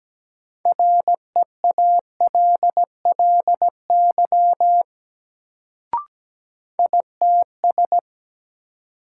Morse code, heard as a keyed tone, about 17 wpm, 700 Hz. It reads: REALLY ITS
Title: Sentences from Top 600 Words - New Word Drill - 17wpm